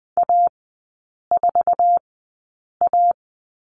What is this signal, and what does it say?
Morse code, heard as a keyed tone, about 20 wpm, spaced as for 8 wpm, 700 Hz. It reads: A4A
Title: 047 Lesson 9e - Number 4 - Three Characters 20wpm